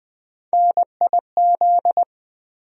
Morse code, heard as a keyed tone, about 20 wpm, 700 Hz. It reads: NIZ
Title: USA Call Sign Postfixes - 3 Letters 20wpm